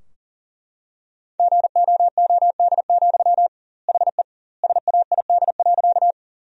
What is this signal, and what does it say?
Morse code, heard as a keyed tone, about 40 wpm, 700 Hz. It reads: GOOD, HE SAID.
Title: 2x Word-Spacing - Sentences from Top 200 Words - 40wpm